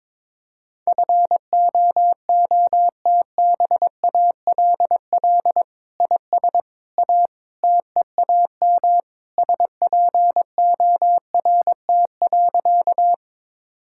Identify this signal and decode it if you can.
Morse code, heard as a keyed tone, about 21 wpm, 700 Hz. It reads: FOOTBALL IS A TEAM SPORT.
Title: Sentences from Top 1600 Words 22wpm